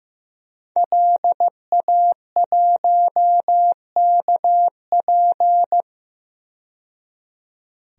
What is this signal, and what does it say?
Morse code, heard as a keyed tone, about 15 wpm, 700 Hz. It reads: LA1KP
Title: DXCC-100 Call Signs - 5 Characters - 15wpm